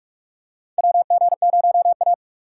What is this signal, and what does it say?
Morse code, heard as a keyed tone, about 46 wpm, 700 Hz. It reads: WG0A
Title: USA Call Signs - 4 Characters - 6x Character Spacing - 45wpm